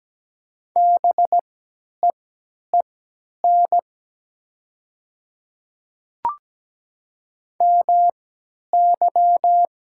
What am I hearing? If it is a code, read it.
Morse code, heard as a keyed tone, about 17 wpm, spaced as for 10 wpm, 700 Hz. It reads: BEEN MY
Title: 3x Character Spacing - Top 300 Words - 17wpm